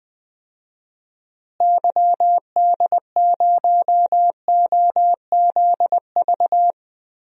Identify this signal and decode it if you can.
Morse code, heard as a keyed tone, about 21 wpm, 700 Hz. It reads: YD0OZV
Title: DXCC-100 Call Signs - 6 Characters - Repeated 2x - 20wpm